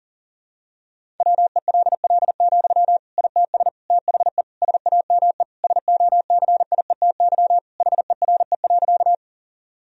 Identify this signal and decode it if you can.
Morse code, heard as a keyed tone, about 37 wpm, 700 Hz. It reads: WELL, ITS THE SAME SOCIETY HERE.
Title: Sentences from Top 1000 Words - Review Part 4 - 40wpm